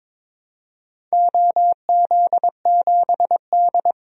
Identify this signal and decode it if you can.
Morse code, heard as a keyed tone, about 23 wpm, 700 Hz. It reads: OZ7D
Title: DXCC-100 Call Signs - 4 Characters - 22wpm